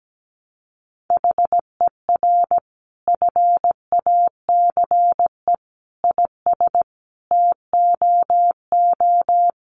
Morse code, heard as a keyed tone, about 17 wpm, 700 Hz. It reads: HER FACE IS TOO
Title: Sentences from Top 1200 Words 17wpm